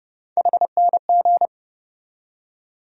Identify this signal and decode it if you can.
Morse code, heard as a keyed tone, about 30 wpm, 700 Hz. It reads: HNG
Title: USA Call Sign Postfixes - 3 Letters 30wpm